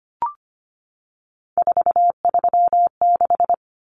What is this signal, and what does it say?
Morse code, heard as a keyed tone, about 25 wpm, 700 Hz. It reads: 436
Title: Common Cut Numbers - 3-Digit 25wpm